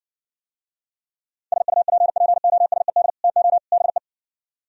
Morse code, heard as a keyed tone, about 59 wpm, 700 Hz. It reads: SUPPOSED TO BE